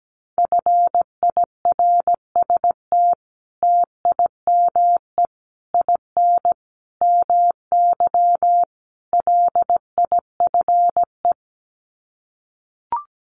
Morse code, heard as a keyed tone, about 17 wpm, 700 Hz. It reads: FIRST TIME IN MY LIFE
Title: Sets of 5 Words Encore2 17wpm